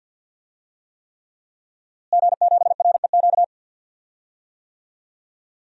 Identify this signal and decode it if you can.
Morse code, heard as a keyed tone, about 50 wpm, 700 Hz. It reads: G7REQ
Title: DXCC-100 Call Signs - 5 Characters - Repeated 3x - 50wpm